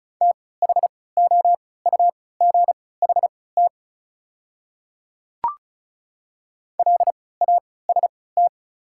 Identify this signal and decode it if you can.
Morse code, heard as a keyed tone, about 36 wpm, spaced as for 20 wpm, 700 Hz. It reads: THOUGHT LAST
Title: 3x Character Spacing - Top 300 Words - 35wpm